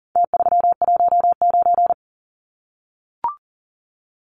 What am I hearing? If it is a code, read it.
Morse code, heard as a keyed tone, about 40 wpm, 700 Hz. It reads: T319
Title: Numbers - Contest Style - 4-Digit 40wpm